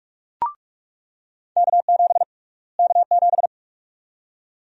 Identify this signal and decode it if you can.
Morse code, heard as a keyed tone, about 45 wpm, 700 Hz. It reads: K7 K7
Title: USA Call Sign Prefixes - Repeated 2x - 45wpm